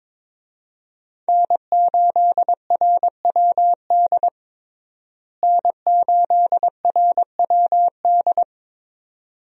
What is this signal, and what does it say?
Morse code, heard as a keyed tone, about 22 wpm, 700 Hz. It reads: N8RWD N8RWD
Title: USA Call Signs - 5 Characters - Repeated 2x - 22wpm